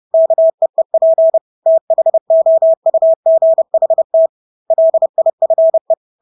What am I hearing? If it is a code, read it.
Morse code, heard as a keyed tone, about 30 wpm, 650 Hz. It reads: KEEP THOUGHT LIFE
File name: Engelske ord 30-30 wpm